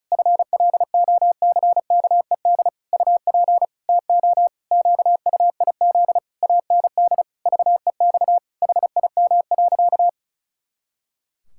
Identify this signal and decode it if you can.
Morse code, heard as a keyed tone, about 36 wpm, 700 Hz. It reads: FLOCKED UP TO QUIZ AND VEX HIM.
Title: Pangram Challenge v2 - 35wpm